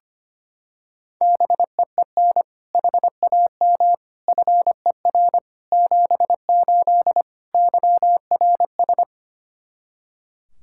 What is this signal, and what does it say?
Morse code, heard as a keyed tone, about 25 wpm, 700 Hz. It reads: BEEN HAM FER 78 YRS
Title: K1USN SST QSO Practice 25wpm